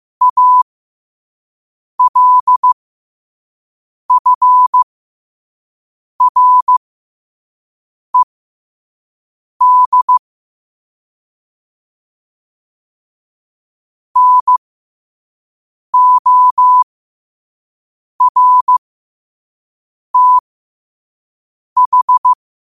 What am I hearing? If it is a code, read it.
Morse code, heard as a keyed tone, about 15 wpm, spaced as for 5 wpm, 1000 Hz. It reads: ALFRED NORTH
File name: qotdfiles / 5wpm